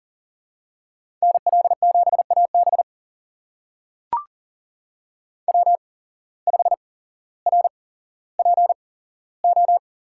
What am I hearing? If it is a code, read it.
Morse code, heard as a keyed tone, about 40 wpm, spaced as for 17 wpm, 700 Hz. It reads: NL7AB W5RPO
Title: USA Call Signs - 5 Characters - 8x Character Spacing - 40wpm